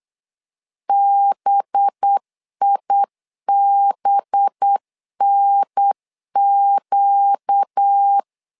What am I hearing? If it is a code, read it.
Morse code, heard as a keyed tone, about 8 wpm, 800 Hz. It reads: BIBNQ
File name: LEZIONI / 5 LBVC / 5 LB 8wpm 2816